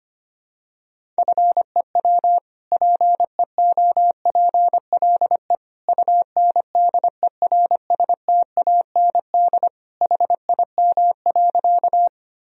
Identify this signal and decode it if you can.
Morse code, heard as a keyed tone, about 25 wpm, 700 Hz. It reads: FEW PEOPLE UNDERSTAND HIM.
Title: Sentences from Top 1000 Words - Review Part 1 - 25wpm